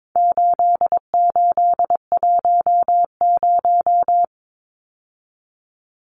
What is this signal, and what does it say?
Morse code, heard as a keyed tone, about 22 wpm, 700 Hz. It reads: 8810
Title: Numbers - 4-Digit 22wpm